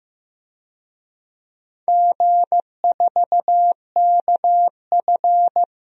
Morse code, heard as a keyed tone, about 15 wpm, 700 Hz. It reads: G4KF
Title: DXCC-100 Call Signs - 4 Characters - 15wpm